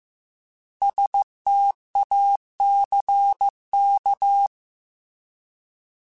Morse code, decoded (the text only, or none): STACK